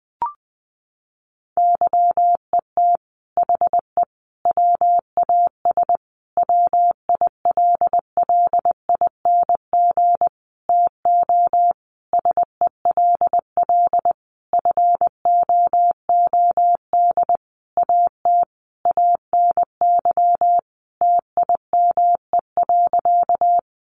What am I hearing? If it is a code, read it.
Morse code, heard as a keyed tone, about 20 wpm, 700 Hz. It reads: YET HE WAS WILLING TO SELL FOOD AT ANY TIME.
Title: Sentences from Top 1100 Words 20wpm